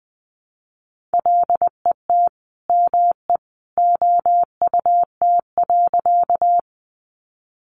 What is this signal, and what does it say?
Morse code, heard as a keyed tone, about 20 wpm, 700 Hz. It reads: LET ME OUT.
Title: Sentences from Top 300 Words 20wpm